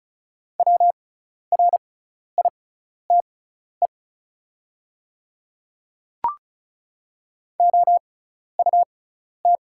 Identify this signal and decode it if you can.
Morse code, heard as a keyed tone, about 36 wpm, spaced as for 12 wpm, 700 Hz. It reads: WRITE OUT
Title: 6x Character Spacing - Top 200 Words - 35wpm